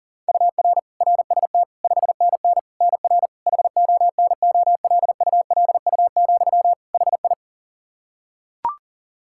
Morse code, heard as a keyed tone, about 40 wpm, 700 Hz. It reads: UR RST 5NN NR HONOLULU, HI
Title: Practice QSO - Exchange 1 - Beginning - 40wpm